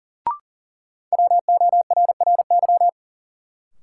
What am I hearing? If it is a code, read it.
Morse code, heard as a keyed tone, about 40 wpm, 700 Hz. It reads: WORRY
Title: Sentences from Top 1100 Words - New Word Drill - 40wpm